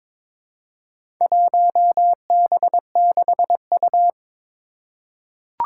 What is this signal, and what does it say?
Morse code, heard as a keyed tone, about 21 wpm, 700 Hz. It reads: 1B6U
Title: Four Challenging Letter-Number Combinations 22wpm